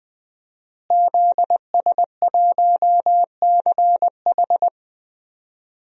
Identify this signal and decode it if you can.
Morse code, heard as a keyed tone, about 21 wpm, 700 Hz. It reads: ZS1CH
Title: DXCC-100 Call Signs - 5 Characters - Repeated 3x - 20wpm